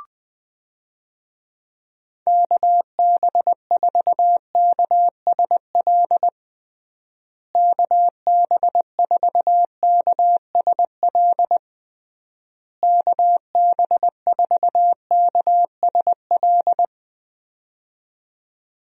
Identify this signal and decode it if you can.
Morse code, heard as a keyed tone, about 20 wpm, 700 Hz. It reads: KB4KSL KB4KSL KB4KSL
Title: USA Call Signs - 6 Characters - Repeated 3x - 20wpm